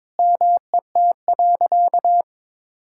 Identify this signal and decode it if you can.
Morse code, heard as a keyed tone, about 22 wpm, 700 Hz. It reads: MET.
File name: Sentences with 7-Word Phrases - 22wpm